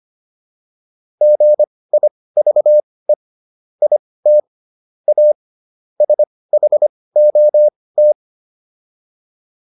Morse code, heard as a keyed tone, about 25 wpm, spaced as for 18 wpm, 600 Hz. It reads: GIVE IT A SHOT